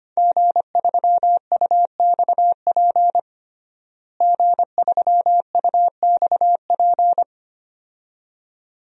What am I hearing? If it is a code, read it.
Morse code, heard as a keyed tone, about 25 wpm, 700 Hz. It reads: G3UXP G3UXP
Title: DXCC-100 Call Signs - 5 Characters - Repeated 2x - 25wpm